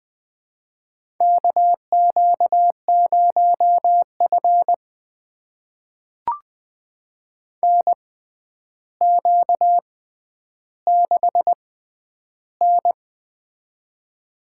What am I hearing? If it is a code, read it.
Morse code, heard as a keyed tone, about 21 wpm, spaced as for 10 wpm, 700 Hz. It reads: KQ0F NQ6N
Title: USA Call Signs - 4 Characters - 6x Character Spacing - 20wpm